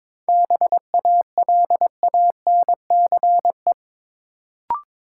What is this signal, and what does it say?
Morse code, heard as a keyed tone, about 22 wpm, 700 Hz. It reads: BALANCE
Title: Sentences from Top 900 Words - New Word Drill - 22wpm